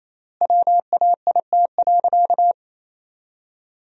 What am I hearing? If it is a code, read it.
Morse code, heard as a keyed tone, about 28 wpm, 700 Hz. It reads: WAIT.
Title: 4x Word-Spacing - Sentences from Top 500 Words - 28wpm